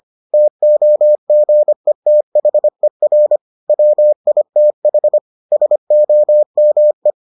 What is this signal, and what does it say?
Morse code, heard as a keyed tone, about 25 wpm, 600 Hz. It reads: TOGETHER WITH SOME